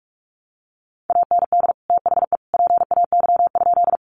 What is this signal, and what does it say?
Morse code, heard as a keyed tone, about 44 wpm, 700 Hz. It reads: AND THE PAY?
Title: Sentences from Top 700 Words 45wpm